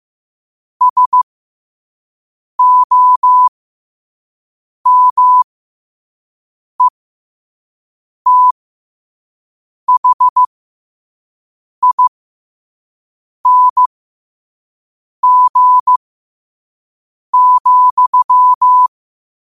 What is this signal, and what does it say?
Morse code, heard as a keyed tone, about 15 wpm, spaced as for 5 wpm, 1000 Hz. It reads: SOMETHING,
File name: qotdfiles / 5wpm